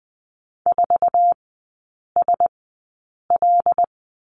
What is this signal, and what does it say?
Morse code, heard as a keyed tone, about 20 wpm, spaced as for 8 wpm, 700 Hz. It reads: 4SL
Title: 081 Lesson 14e - Number 2 - Three Characters 20wpm